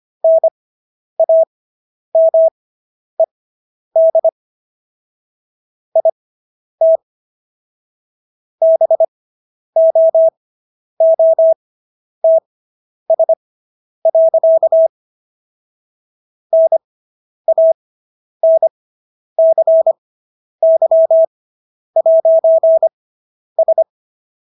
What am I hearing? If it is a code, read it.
Morse code, heard as a keyed tone, about 25 wpm, spaced as for 10 wpm, 650 Hz. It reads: NAMED IT BOOTS. NANCY'S